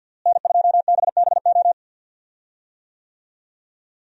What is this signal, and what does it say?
Morse code, heard as a keyed tone, about 49 wpm, 700 Hz. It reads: N2BBO
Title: USA Call Signs - 5 Characters - Repeated 3x - 50wpm